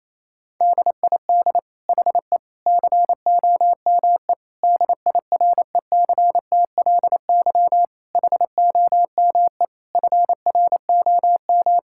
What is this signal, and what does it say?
Morse code, heard as a keyed tone, about 28 wpm, 700 Hz. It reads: DID HE COME DIRECTLY HOME FROM
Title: Sentences from Top 1000 Words - Review Part 5 - 28wpm